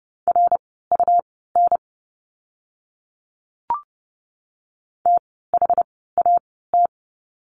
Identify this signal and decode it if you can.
Morse code, heard as a keyed tone, about 30 wpm, spaced as for 17 wpm, 700 Hz. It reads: RUN THAT
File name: course / 30:10wpm / 115 Lesson 19f - Letter w - Words 30wpm